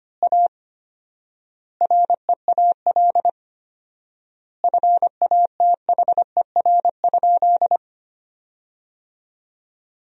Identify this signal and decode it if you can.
Morse code, heard as a keyed tone, about 25 wpm, 700 Hz. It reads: A REAL FATHER?
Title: 4x Word-Spacing - Sentences from Top 300 Words - 25wpm